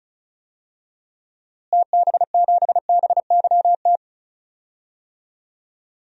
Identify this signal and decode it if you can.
Morse code, heard as a keyed tone, about 36 wpm, 700 Hz. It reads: TB7BYT
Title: DXCC-100 Call Signs - 6 Characters - Repeated 3x - 35wpm